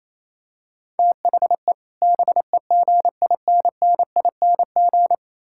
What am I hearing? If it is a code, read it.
Morse code, heard as a keyed tone, about 28 wpm, 700 Hz. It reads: THE BEGINNING
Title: Sentences from Top 1000 Words - Review Part 6 - 28wpm